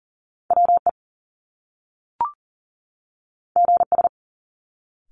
Morse code, heard as a keyed tone, about 40 wpm, 700 Hz. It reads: WE GS
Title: Top Two Letter Combinations in Words 40wpm